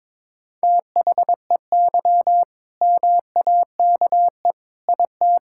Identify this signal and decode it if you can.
Morse code, heard as a keyed tone, about 22 wpm, 690 Hz. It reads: THEY MAKE IT
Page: morse-code-ninja-practice-22wpm